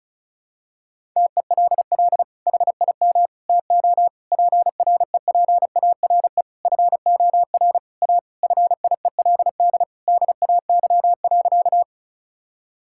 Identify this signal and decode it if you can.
Morse code, heard as a keyed tone, about 34 wpm, 700 Hz. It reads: TELL HIM TO PREPARE FOR A FIELD DAY.
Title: Sentences from Top 1000 Words - Review Part 6 - 35wpm